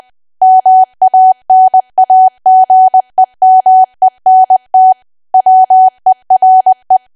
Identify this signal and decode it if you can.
Morse code, heard as a keyed tone, about 20 wpm, 750 Hz. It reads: MANAGEMENT WERE
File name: Archive / 20 WPM / 20WPM